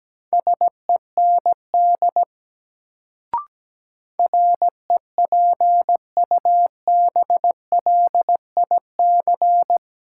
Morse code, heard as a keyed tone, about 17 wpm, 700 Hz. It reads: SEND REPUBLIC
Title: Sentences from Top 1500 Words - New Word Drill Alt - 17wpm